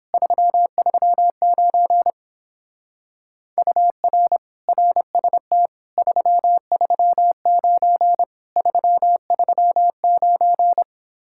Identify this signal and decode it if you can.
Morse code, heard as a keyed tone, about 31 wpm, 700 Hz. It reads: 339 UR RST 339 339